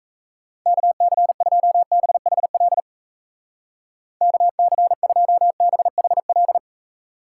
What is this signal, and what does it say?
Morse code, heard as a keyed tone, about 36 wpm, 700 Hz. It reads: KC2BHL KC2BHL